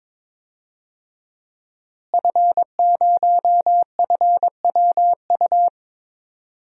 Morse code, heard as a keyed tone, about 22 wpm, 700 Hz. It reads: F0FWU
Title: DXCC-100 Call Signs - 5 Characters - Repeated 2x - 22wpm